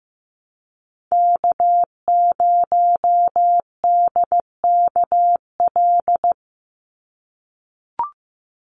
Morse code, heard as a keyed tone, about 15 wpm, 700 Hz. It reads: K0DKL
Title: USA Call Signs 15wpm